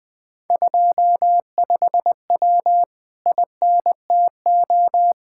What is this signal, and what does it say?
Morse code, heard as a keyed tone, about 20 wpm, 700 Hz. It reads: 25W INTO
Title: Speed-Racing Common QSO Phrases Encore - 17wpm